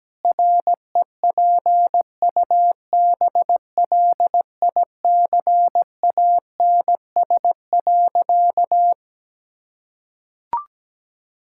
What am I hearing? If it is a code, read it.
Morse code, heard as a keyed tone, about 17 wpm, 700 Hz. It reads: REPUBLICANS.
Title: Pangram Challenge v2 - 17wpm